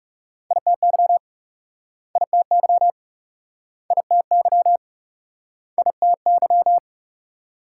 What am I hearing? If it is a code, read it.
Morse code, heard as a keyed tone, about 43 wpm, 700 Hz. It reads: ITY ITY ITY ITY